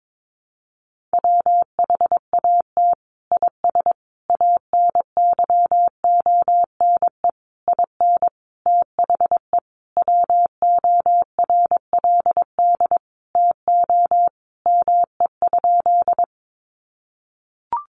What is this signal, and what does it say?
Morse code, heard as a keyed tone, about 22 wpm, 700 Hz. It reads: WHAT IS ANYONE IN THE WORLD TO ME?
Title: Sentences from Top 900 Words 22wpm